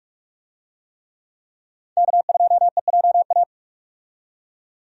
Morse code, heard as a keyed tone, about 47 wpm, 700 Hz. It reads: K2EJA
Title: USA Call Signs - 5 Characters - 45wpm